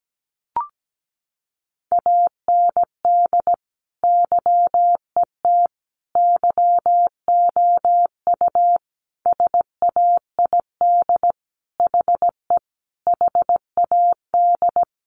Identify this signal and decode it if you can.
Morse code, heard as a keyed tone, about 17 wpm, 700 Hz. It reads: AND YET YOU SAID HE HAD
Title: Sentences from Top 500 Words 17wpm